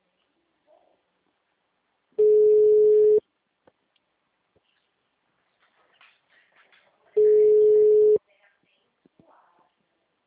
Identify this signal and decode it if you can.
Ringback tone (Europe, 425 Hz), with speech.
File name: out- -2012- - - .246